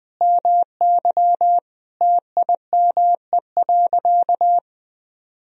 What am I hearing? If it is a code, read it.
Morse code, heard as a keyed tone, about 20 wpm, 700 Hz. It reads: MY TIME.
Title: Sentences from Top 1000 Words - Review Part 1 - 20wpm